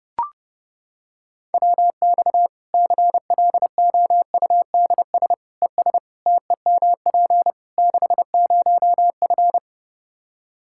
Morse code, heard as a keyed tone, about 30 wpm, 700 Hz. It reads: WX CLOUDS ES TEMP 60F
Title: Practice QSO - Exchange 2 - Info - 30wpm